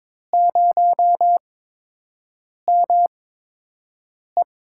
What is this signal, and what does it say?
Morse code, heard as a keyed tone, about 21 wpm, spaced as for 6 wpm, 700 Hz. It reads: 0ME
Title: USA Call Signs - 8x Character Spacing - 22wpm